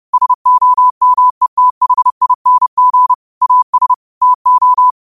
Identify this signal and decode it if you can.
Morse code, heard as a keyed tone, about 30 wpm, 1000 Hz. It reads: SOMETHING AS TO
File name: qotdfiles / 30wpm